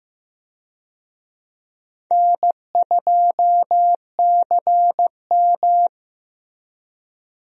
Morse code, heard as a keyed tone, about 15 wpm, 700 Hz. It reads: N2CM
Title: USA Call Signs - 4 Characters - Repeated 3x - 15wpm